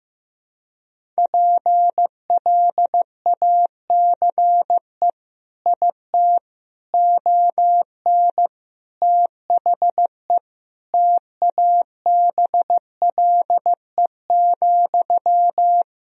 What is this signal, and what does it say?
Morse code, heard as a keyed tone, about 15 wpm, 700 Hz. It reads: PLACE IT ON THE TABLE,
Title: Sentences from Top 1000 Words - Review Part 2 - 15wpm